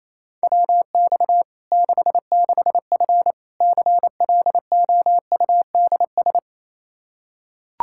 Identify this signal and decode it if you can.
Morse code, heard as a keyed tone, about 28 wpm, 700 Hz. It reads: WX 66F CLOUDS
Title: Practice QSO - Exchange 2 - Info - 28wpm